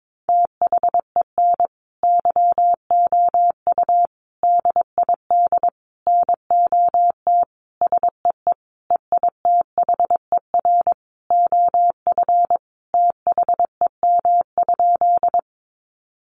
Morse code, heard as a keyed tone, about 22 wpm, 700 Hz. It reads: THEN YOU DID NOT SEE EITHER OF THEM?
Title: Sentences from Top 600 Words 22wpm